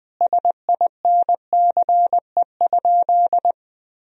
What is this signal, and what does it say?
Morse code, heard as a keyed tone, about 20 wpm, 700 Hz. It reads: SINCE?
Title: Sentences from Top 1000 Words - Review Part 4 - 20wpm